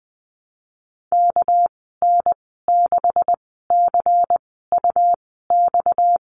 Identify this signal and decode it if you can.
Morse code, heard as a keyed tone, about 20 wpm, spaced as for 14 wpm, 700 Hz. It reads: KN6CUX
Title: 231 Lesson 37g - Letter x - Call Signs 20wpm